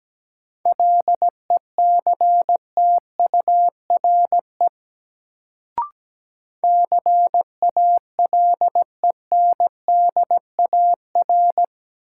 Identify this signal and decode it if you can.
Morse code, heard as a keyed tone, about 17 wpm, 700 Hz. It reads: LECTURE CALENDAR
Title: Sentences from Top 1600 Words - New Word Drill Alt - 17wpm